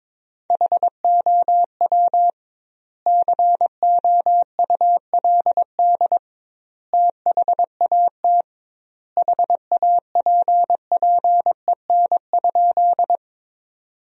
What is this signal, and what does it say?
Morse code, heard as a keyed tone, about 22 wpm, 700 Hz. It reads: HOW COULD THAT HAPPEN?
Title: 2x Word-Spacing - Sentences from Top 400 Words - 22wpm